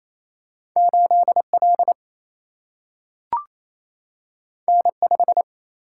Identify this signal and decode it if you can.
Morse code, heard as a keyed tone, about 28 wpm, 700 Hz. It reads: 8L N5
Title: Single-Letter Single-Number 28wpm